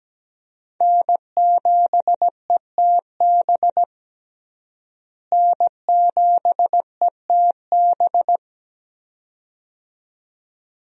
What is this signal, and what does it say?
Morse code, heard as a keyed tone, about 17 wpm, 700 Hz. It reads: N7ETB N7ETB
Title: USA Call Signs - 5 Characters - Repeated 2x - 17wpm